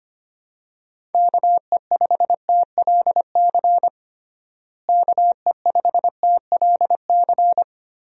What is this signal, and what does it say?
Morse code, heard as a keyed tone, about 25 wpm, 700 Hz. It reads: KE5TLC KE5TLC
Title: USA Call Signs - 6 Characters - Repeated 3x - 25wpm